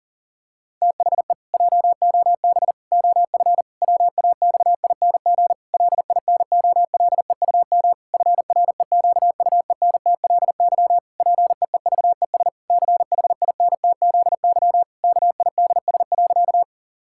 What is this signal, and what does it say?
Morse code, heard as a keyed tone, about 40 wpm, 700 Hz. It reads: THE JOB OF WAXING LINOLEUM FREQUENTLY PEEVES CHINTZY KIDS.
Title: Pangram Challenge v2 - 40wpm